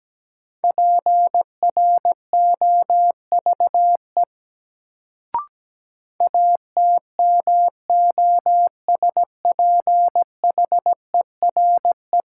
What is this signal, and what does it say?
Morse code, heard as a keyed tone, about 17 wpm, 700 Hz. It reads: PROVE ATMOSPHERE
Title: Sentences from Top 1600 Words - New Word Drill Alt - 17wpm